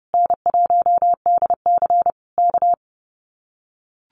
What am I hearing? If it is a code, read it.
Morse code, heard as a keyed tone, about 30 wpm, 700 Hz. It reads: N1DC K
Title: Common QSO Phrases Encore 30wpm